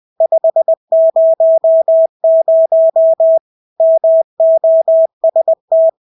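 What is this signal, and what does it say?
Morse code, heard as a keyed tone, about 20 wpm, 650 Hz. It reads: 500 MOST
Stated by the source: Morse practice of English words at 20 wpm.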